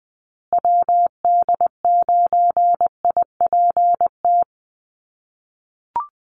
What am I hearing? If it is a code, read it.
Morse code, heard as a keyed tone, about 20 wpm, 700 Hz. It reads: WD9IPT
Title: USA Call Signs Repeated 3x 20wpm